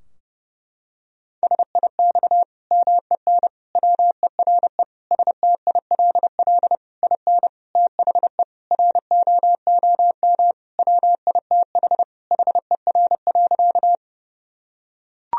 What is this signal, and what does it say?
Morse code, heard as a keyed tone, about 32 wpm, 700 Hz. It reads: SIX MEN WERE STILL IN THE ROOM WITH HER.
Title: Sentences from Top 1000 Words - Review Part 2 - 30wpm